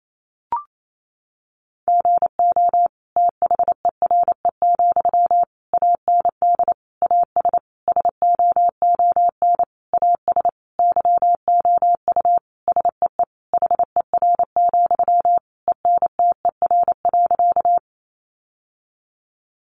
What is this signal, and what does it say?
Morse code, heard as a keyed tone, about 28 wpm, 700 Hz. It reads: GO THERE, AND AS SOON AS YOU SEE HER, ENTER.
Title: Sentences from Top 1300 Words 28wpm